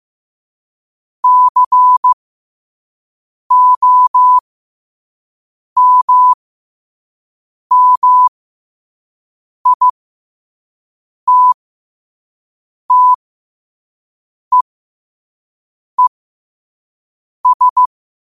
Morse code, heard as a keyed tone, about 15 wpm, spaced as for 5 wpm, 1000 Hz. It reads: COMMITTEES
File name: qotdfiles / 5wpm